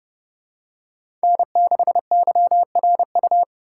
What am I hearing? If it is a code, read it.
Morse code, heard as a keyed tone, about 29 wpm, 700 Hz. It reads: N6YRU